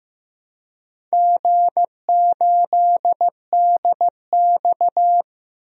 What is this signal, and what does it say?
Morse code, heard as a keyed tone, about 15 wpm, 700 Hz. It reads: G8DX
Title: DXCC-100 Call Signs - 4 Characters - 15wpm